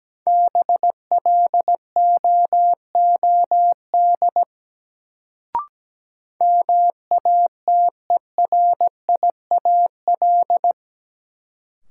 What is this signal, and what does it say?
Morse code, heard as a keyed tone, about 17 wpm, 700 Hz. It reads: BLOOD MATERIAL
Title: Sentences from Top 1100 Words - New Word Drill Alt - 17wpm